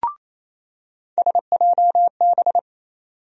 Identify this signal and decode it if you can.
Morse code, heard as a keyed tone, about 28 wpm, 700 Hz. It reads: SJB
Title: USA Call Sign Postfixes - 3 Letters 28wpm